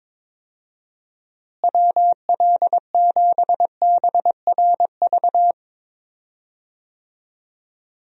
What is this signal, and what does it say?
Morse code, heard as a keyed tone, about 22 wpm, 700 Hz. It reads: WL7BRV
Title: USA Call Signs - 6 Characters - Repeated 2x - 22wpm